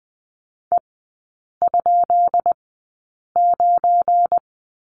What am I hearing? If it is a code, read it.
Morse code, heard as a keyed tone, about 20 wpm, spaced as for 8 wpm, 700 Hz. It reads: E?9
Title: 198 Lesson 32e - Letter k - Three Characters 20wpm